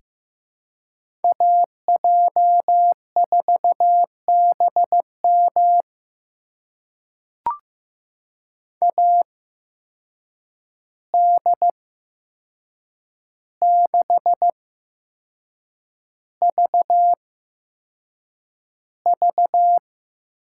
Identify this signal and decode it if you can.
Morse code, heard as a keyed tone, about 15 wpm, spaced as for 6 wpm, 700 Hz. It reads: AJ4BM AD6VV
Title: USA Call Signs - 5 Characters - 8x Character Spacing - 15wpm